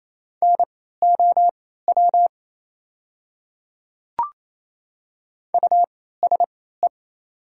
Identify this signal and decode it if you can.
Morse code, heard as a keyed tone, about 28 wpm, spaced as for 16 wpm, 700 Hz. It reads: NOW USE
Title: 3x Character Spacing - Top 200 Words - 28wpm